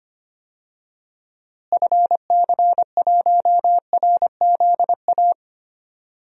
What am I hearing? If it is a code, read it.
Morse code, heard as a keyed tone, about 25 wpm, 700 Hz. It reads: FC1RZA
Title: DXCC-100 Call Signs - 6 Characters - Repeated 2x - 25wpm